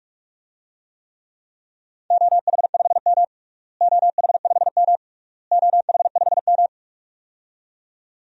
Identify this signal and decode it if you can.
Morse code, heard as a keyed tone, about 46 wpm, 700 Hz. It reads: OH5M OH5M OH5M
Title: DXCC-100 Call Signs - 4 Characters - Repeated 3x - 45wpm